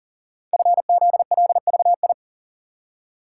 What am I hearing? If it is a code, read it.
Morse code, heard as a keyed tone, about 39 wpm, 700 Hz. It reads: FZLVI
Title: Code Groups - 5 Letters - 40wpm